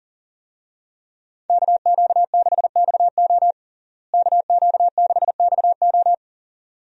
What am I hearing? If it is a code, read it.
Morse code, heard as a keyed tone, about 40 wpm, 700 Hz. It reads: KQ6XO KQ6XO
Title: USA Call Signs - 5 Characters - Repeated 2x - 40wpm